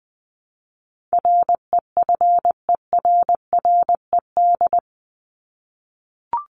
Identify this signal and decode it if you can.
Morse code, heard as a keyed tone, about 20 wpm, 700 Hz. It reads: REFERRED
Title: ED Suffix - Top 100 Words with Base Words 20wpm